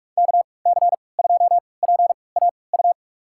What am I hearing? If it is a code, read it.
Morse code, heard as a keyed tone, about 45 wpm, spaced as for 25 wpm, 690 Hz. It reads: KC2PAU
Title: USA Call Signs - 6 Characters - 3x Character Spacing - 45wpm